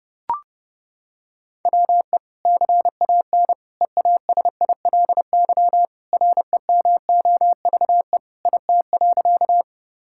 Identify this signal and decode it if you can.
Morse code, heard as a keyed tone, about 30 wpm, 700 Hz. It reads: WE CAN EASILY REMOVE IT.
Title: Sentences from Top 1000 Words - Review Part 5 - 30wpm